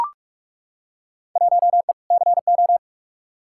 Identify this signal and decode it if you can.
Morse code, heard as a keyed tone, about 43 wpm, 700 Hz. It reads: 1E CO
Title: Field Day Exchanges - 45wpm